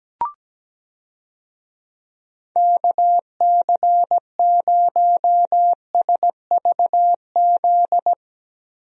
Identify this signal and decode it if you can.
Morse code, heard as a keyed tone, about 17 wpm, 700 Hz. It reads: KC0SVZ
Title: USA Call Signs - 6 Characters - Repeated 3x - 17wpm